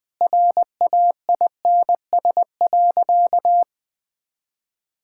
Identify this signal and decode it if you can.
Morse code, heard as a keyed tone, about 20 wpm, 700 Hz. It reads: RAINS.
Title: Sentences with 5-Word Phrases - 20wpm